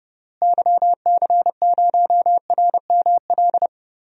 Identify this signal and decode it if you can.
Morse code, heard as a keyed tone, about 29 wpm, 700 Hz. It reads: YC0RML
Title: DXCC-100 Call Signs - 6 Characters - Repeated 3x - 30wpm